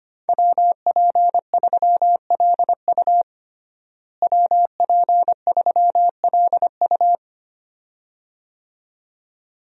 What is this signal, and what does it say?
Morse code, heard as a keyed tone, about 25 wpm, 700 Hz. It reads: WP3LU WP3LU
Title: USA Call Signs - 5 Characters - Repeated 2x - 25wpm